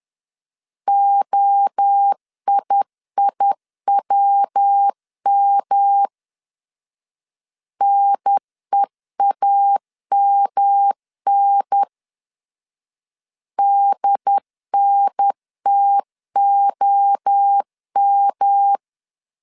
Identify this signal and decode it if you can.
Morse code, heard as a keyed tone, about 10 wpm, 800 Hz. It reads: OIIWM NEAMN DNTOM
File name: CW / DET / 1 3 1500C S C 10wpm 3335